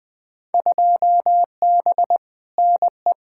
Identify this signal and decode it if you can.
Morse code, heard as a keyed tone, about 21 wpm, 700 Hz. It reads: 2B NE
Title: Field Day Exchanges Encore - 20wpm